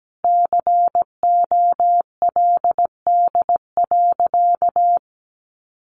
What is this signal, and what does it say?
Morse code, heard as a keyed tone, about 17 wpm, 700 Hz. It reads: COLD.
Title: Sentences from Top 1300 Words 17wpm